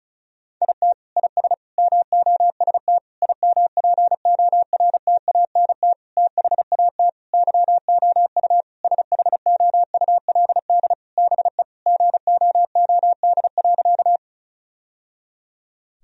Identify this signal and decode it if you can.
Morse code, heard as a keyed tone, about 36 wpm, 700 Hz. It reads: IT IS MOST IMPORTANT THAT YOU SHOULD BE GOOD.
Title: Sentences from Top 1000 Words - Review Part 1 - 35wpm